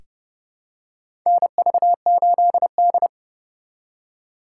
Morse code, heard as a keyed tone, about 32 wpm, 700 Hz. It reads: NV8D
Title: USA Call Signs - 4x Character Spacing - 30wpm